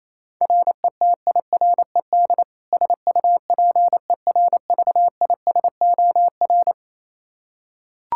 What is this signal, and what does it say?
Morse code, heard as a keyed tone, about 28 wpm, 700 Hz. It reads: RETIRED SUPERVISOR
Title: K1USN SST QSO Practice 28wpm